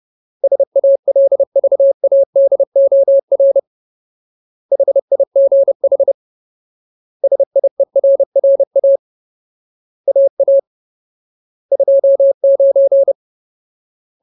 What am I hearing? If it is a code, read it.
Morse code, heard as a keyed tone, about 30 wpm, 550 Hz. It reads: SALVADOR HIGH SIERRA AA 29
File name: Ultimate Word List / 30WPM Ultimate